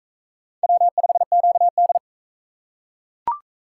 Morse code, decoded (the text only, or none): W5QD